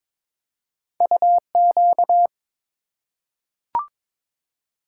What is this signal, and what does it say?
Morse code, heard as a keyed tone, about 22 wpm, 700 Hz. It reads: UQ
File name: practice / 22wpm / Two Letter-Number Combinations - 22wpm